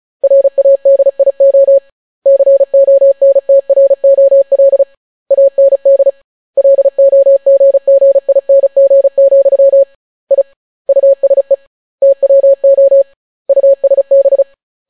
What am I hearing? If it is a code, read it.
Morse code, heard as a keyed tone, about 36 wpm, 550 Hz. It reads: RADIO CONTROL AND LOGGING, I USE TWO USB